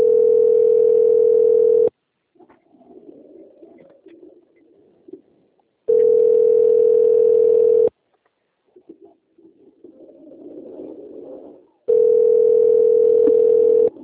Ringback tone (North America), 440 + 480 Hz.
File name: out- - - - - .10215